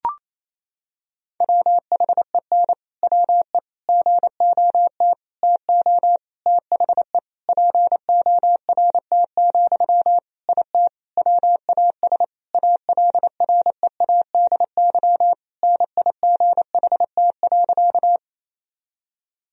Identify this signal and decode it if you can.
Morse code, heard as a keyed tone, about 28 wpm, 700 Hz. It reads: WHEN WE GOT TO THE PORT, IT WAS ALREADY NIGHT.
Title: Sentences from Top 1000 Words - Review Part 1 - 28wpm